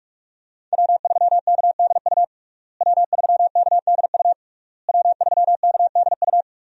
Morse code, heard as a keyed tone, about 45 wpm, 700 Hz. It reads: W3KDU W3KDU W3KDU
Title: USA Call Signs - 5 Characters - Repeated 3x - 45wpm